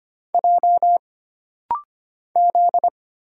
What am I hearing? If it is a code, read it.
Morse code, heard as a keyed tone, about 25 wpm, 700 Hz. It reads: J Z
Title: Single Letters - Rapid-Fire 25wpm